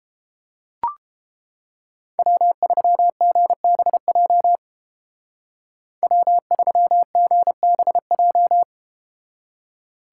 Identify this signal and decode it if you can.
Morse code, heard as a keyed tone, about 31 wpm, 700 Hz. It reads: W3GBJ W3GBJ